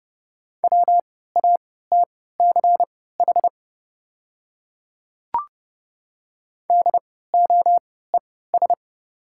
Morse code, heard as a keyed tone, about 30 wpm, spaced as for 17 wpm, 690 Hz. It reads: WATCH DOES
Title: 3x Character Spacing - Top 500 Words - 30wpm